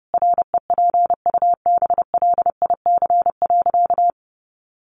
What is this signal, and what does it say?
Morse code, heard as a keyed tone, about 30 wpm, 700 Hz. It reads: REPUBLIC.
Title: Sentences from Top 1500 Words 30wpm